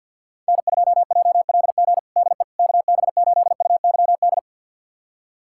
Morse code, heard as a keyed tone, about 46 wpm, 700 Hz. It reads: N2JLG DE KB8UYD
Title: USA Callsign DE Callsign 50wpm